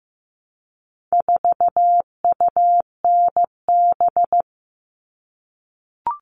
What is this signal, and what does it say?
Morse code, heard as a keyed tone, about 15 wpm, 700 Hz. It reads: 4UNB
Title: All Cut Numbers - 4-Digit 15wpm